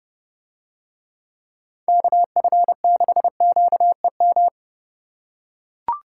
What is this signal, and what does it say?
Morse code, heard as a keyed tone, about 31 wpm, 700 Hz. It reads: KF6QEM